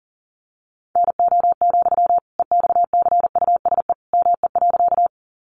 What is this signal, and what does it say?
Morse code, heard as a keyed tone, about 40 wpm, 700 Hz. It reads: NO, EXCUSE ME.
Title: Sentences from Top 700 Words 40wpm